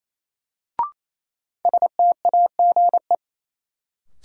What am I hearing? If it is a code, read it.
Morse code, heard as a keyed tone, about 28 wpm, 700 Hz. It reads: STAGE
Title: Sentences from Top 1200 Words - New Word Drill Alt - 28wpm